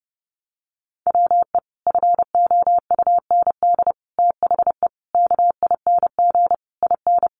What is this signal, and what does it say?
Morse code, heard as a keyed tone, about 30 wpm, 700 Hz. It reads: WE FOUND THE KING IN
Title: Sentences from Top 1300 Words 30wpm